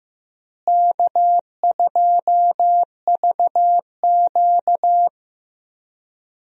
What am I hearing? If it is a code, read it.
Morse code, heard as a keyed tone, about 15 wpm, 700 Hz. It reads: K2VQ
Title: USA Call Signs - 4 Characters - Repeated 2x - 15wpm